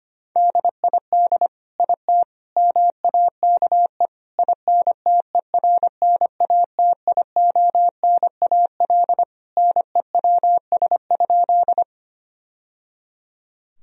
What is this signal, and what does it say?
Morse code, heard as a keyed tone, about 25 wpm, 700 Hz. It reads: DID IT MAKE INTERNATIONAL NEWS?
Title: Sentences from Top 1000 Words - Review Part 4 - 25wpm